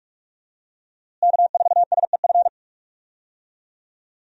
Morse code, heard as a keyed tone, about 42 wpm, 700 Hz. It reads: K4SEF
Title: USA Call Signs - 5 Characters - Repeated 2x - 45wpm